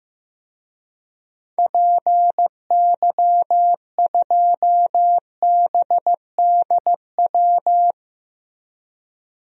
Morse code, heard as a keyed tone, about 15 wpm, 700 Hz. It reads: PY2BDW
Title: DXCC-100 Call Signs - 6 Characters - Repeated 3x - 15wpm